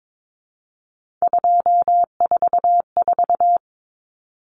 Morse code, heard as a keyed tone, about 22 wpm, 700 Hz. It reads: 244
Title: Numbers - 3-Digit 22wpm